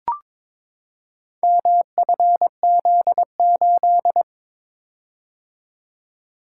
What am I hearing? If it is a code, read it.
Morse code, heard as a keyed tone, about 22 wpm, 700 Hz. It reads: MFZ8
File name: practice / 22wpm / Four Letter-Number Combinations - 22wpm